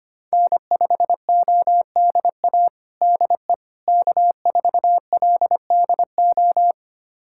Morse code, heard as a keyed tone, about 25 wpm, 700 Hz. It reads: N5ODA DE K4LDO
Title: USA Callsign DE Callsign - 5 Characters - 25wpm